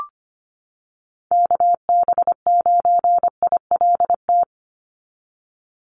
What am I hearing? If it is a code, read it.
Morse code, heard as a keyed tone, about 25 wpm, 700 Hz. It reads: KB9ILT
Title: NA SOTA Activator Callsigns 25wpm